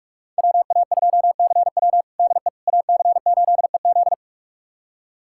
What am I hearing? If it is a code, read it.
Morse code, heard as a keyed tone, about 43 wpm, 700 Hz. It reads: WA2CW DE AC8EZ
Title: USA Callsign DE Callsign - 5 Characters - 45wpm